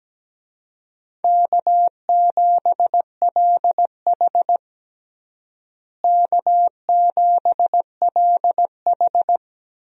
Morse code, heard as a keyed tone, about 17 wpm, 700 Hz. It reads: K7LH K7LH
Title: USA Call Signs - 4 Characters - Repeated 2x - 17wpm